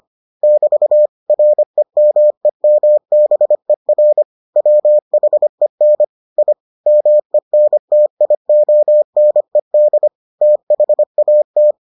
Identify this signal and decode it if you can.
Morse code, heard as a keyed tone, about 25 wpm, 600 Hz. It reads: <BT> REMEMBER WHEN I MENTIONED THAT